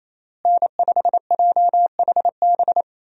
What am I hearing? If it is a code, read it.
Morse code, heard as a keyed tone, about 27 wpm, 700 Hz. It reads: N5JHB
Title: USA Call Signs - 5 Characters - Repeated 3x - 28wpm